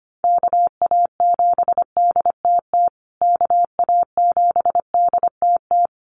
Morse code, heard as a keyed tone, about 25 wpm, 700 Hz. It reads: KA7DTT KA7DTT
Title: USA Call Signs Repeated 2x 25wpm